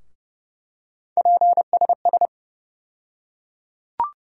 Morse code, heard as a keyed tone, about 29 wpm, 700 Hz. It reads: PSS
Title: USA Call Sign Postfixes - 3 Letters 30wpm